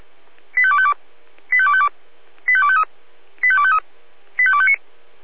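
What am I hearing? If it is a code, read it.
Five-tone selective call (ZVEI), heard as quick runs of five tones, tones of about 70 ms. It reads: 85252 85252 85253 85253 85259
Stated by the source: ZVEI five-tone call received by SDR.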